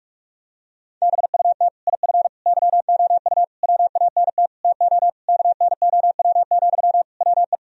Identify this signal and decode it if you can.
Morse code, heard as a keyed tone, about 46 wpm, 700 Hz. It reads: BUT IF YOU WANT TO KNOW, WE
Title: Sentences from Top 1000 Words - Review Part 3 - 45wpm